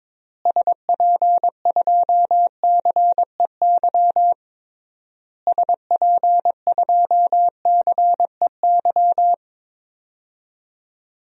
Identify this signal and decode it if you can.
Morse code, heard as a keyed tone, about 21 wpm, 700 Hz. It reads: SP2CEY SP2CEY
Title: DXCC-100 Call Signs - 6 Characters - Repeated 3x - 22wpm